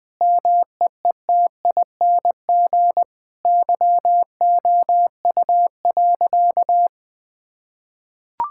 Morse code, heard as a keyed tone, about 20 wpm, 700 Hz. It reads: MEETING YOU.
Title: Sentences with 7-Word Phrases - Spoken Beginning - 20wpm